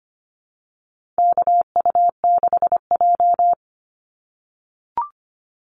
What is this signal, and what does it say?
Morse code, heard as a keyed tone, about 25 wpm, 700 Hz. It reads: KU6J
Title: NA SOTA Activator Callsigns 25wpm